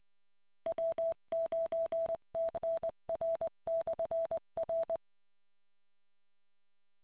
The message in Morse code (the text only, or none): W9TNR/R